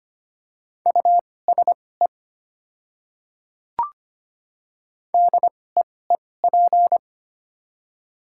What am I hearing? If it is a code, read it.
Morse code, heard as a keyed tone, about 25 wpm, spaced as for 18 wpm, 700 Hz. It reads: USE DEEP
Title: 2x Character Spacing - Top 500 Words - 25wpm